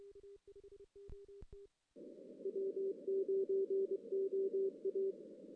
Morse code, heard as a keyed tone, about 23 wpm, 380 Hz. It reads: W9OA
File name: 2016 JAN CW NAQP / 2016 Jan naqp k5ot - 0-11